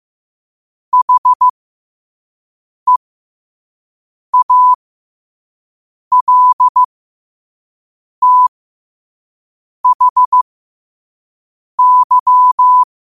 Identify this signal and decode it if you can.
Morse code, heard as a keyed tone, about 15 wpm, spaced as for 5 wpm, 1000 Hz. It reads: HEALTHY
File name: qotdfiles / 5wpm